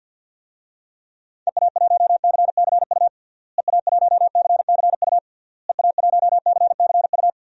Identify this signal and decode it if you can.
Morse code, heard as a keyed tone, about 52 wpm, 700 Hz. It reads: EA1CCU EA1CCU EA1CCU
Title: DXCC-100 Call Signs - 6 Characters - Repeated 3x - 50wpm